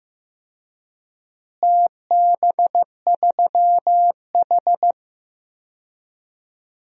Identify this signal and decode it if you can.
Morse code, heard as a keyed tone, about 15 wpm, 700 Hz. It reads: TB3H
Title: DXCC-100 Call Signs - 4 Characters - Repeated 3x - 15wpm